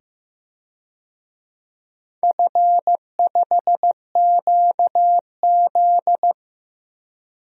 Morse code, heard as a keyed tone, about 15 wpm, 700 Hz. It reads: F5QZ
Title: DXCC-100 Call Signs - 4 Characters - 15wpm